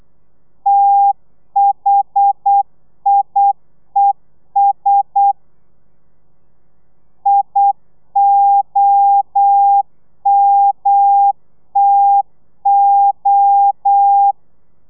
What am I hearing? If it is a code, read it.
Morse code, heard as a keyed tone, about 8 wpm, 800 Hz. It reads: THIES IOMTO
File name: telegrafia / i2tfj / Lez1 35